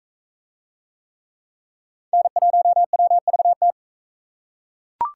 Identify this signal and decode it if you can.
Morse code, heard as a keyed tone, about 42 wpm, 700 Hz. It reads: N1WVT